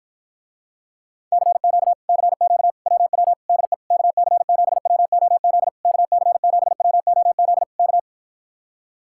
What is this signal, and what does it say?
Morse code, heard as a keyed tone, about 54 wpm, 700 Hz. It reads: CQ CQ WW DE KC7WOZ KC7WOZ K